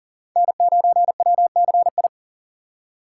Morse code, heard as a keyed tone, about 40 wpm, 700 Hz. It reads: N9WCI
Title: USA Call Signs - 3x Character Spacing - 40wpm